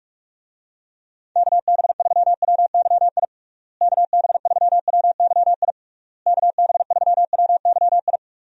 Morse code, heard as a keyed tone, about 46 wpm, 700 Hz. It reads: KB3WYI KB3WYI KB3WYI
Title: USA Call Signs - 6 Characters - Repeated 3x - 45wpm